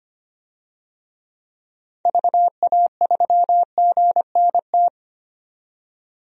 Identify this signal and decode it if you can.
Morse code, heard as a keyed tone, about 25 wpm, 690 Hz. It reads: VA3GNT
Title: DXCC-100 Call Signs - 6 Characters - 25wpm